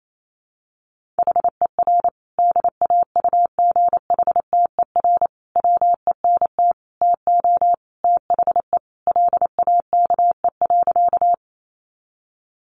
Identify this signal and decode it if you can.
Morse code, heard as a keyed tone, about 28 wpm, 700 Hz. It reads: HER DAUGHTER WENT TO THE LAKE.
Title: Sentences from Top 1500 Words 28wpm